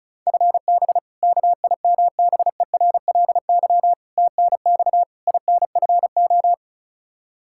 Morse code, heard as a keyed tone, about 36 wpm, 700 Hz. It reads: FB KIMBERLY TNX INFO
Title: Practice QSO - Exchange 3 - More Info - 35wpm